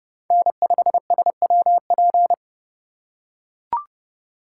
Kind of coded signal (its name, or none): Morse code